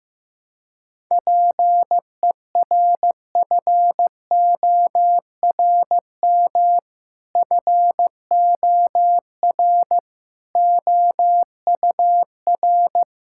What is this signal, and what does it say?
Morse code, heard as a keyed tone, about 16 wpm, 700 Hz. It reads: PERFORM FOR OUR
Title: Sentences from Top 1000 Words - Review Part 5 - 15wpm